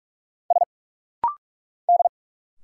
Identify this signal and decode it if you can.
Morse code, heard as a keyed tone, about 44 wpm, 700 Hz. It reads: S D
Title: Single Letter-Number - Rapid-Fire 45wpm